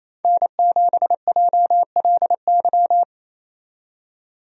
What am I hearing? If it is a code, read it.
Morse code, heard as a keyed tone, about 28 wpm, 700 Hz. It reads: N7JLY
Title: USA Call Signs - 5 Characters - 28wpm